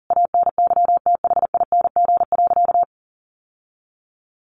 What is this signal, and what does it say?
Morse code, heard as a keyed tone, about 40 wpm, 700 Hz. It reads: ANYTHING.
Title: Sentences from Top 1200 Words 40wpm